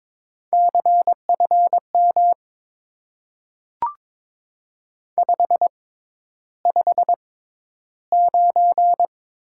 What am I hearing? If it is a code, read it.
Morse code, heard as a keyed tone, about 22 wpm, spaced as for 11 wpm, 700 Hz. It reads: CFM 559
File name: Top 90 QSO Elements - 6x Character Spacing - 22wpm